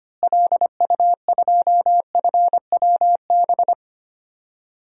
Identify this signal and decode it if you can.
Morse code, heard as a keyed tone, about 25 wpm, 700 Hz. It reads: LU2FWB
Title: DXCC-100 Call Signs - 6 Characters - Repeated 2x - 25wpm